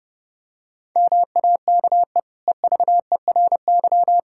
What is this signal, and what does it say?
Morse code, heard as a keyed tone, about 30 wpm, 700 Hz. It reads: MAKE EVERY
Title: Sentences from Top 1000 Words - Review Part 5 - 30wpm